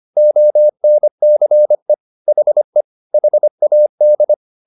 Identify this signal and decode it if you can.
Morse code, heard as a keyed tone, about 25 wpm, 600 Hz. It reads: ONCE HE HAD